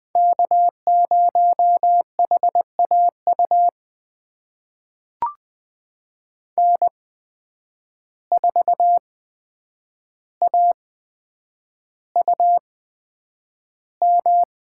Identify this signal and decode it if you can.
Morse code, heard as a keyed tone, about 20 wpm, spaced as for 9 wpm, 700 Hz. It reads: K0HAU N4AUM
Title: USA Call Signs - 5 Characters - 8x Character Spacing - 20wpm